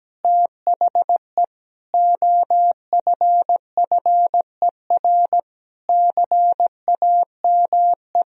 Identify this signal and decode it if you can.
Morse code, heard as a keyed tone, about 17 wpm, 700 Hz. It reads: THE OFFER CAME
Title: Sentences from Top 1000 Words - Review Part 4 - 17wpm